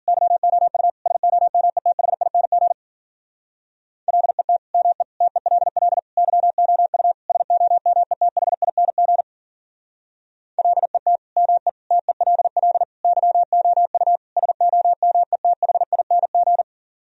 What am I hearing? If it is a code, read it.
Morse code, heard as a keyed tone, about 52 wpm, 700 Hz. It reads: YOU SOMETHING LET ME TELL YOU SOMETHING LET ME TELL YOU SOMETHING